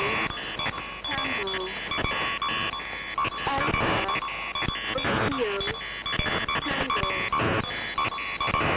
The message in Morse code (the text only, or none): none